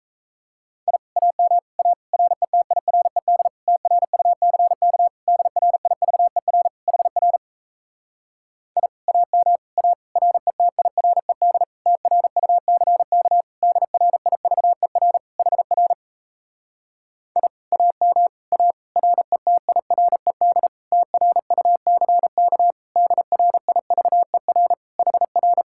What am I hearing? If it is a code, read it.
Morse code, heard as a keyed tone, about 37 wpm, 700 Hz. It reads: I AM A RETIRED TRUCK DRIVER HR I AM A RETIRED TRUCK DRIVER HR I AM A RETIRED TRUCK DRIVER HR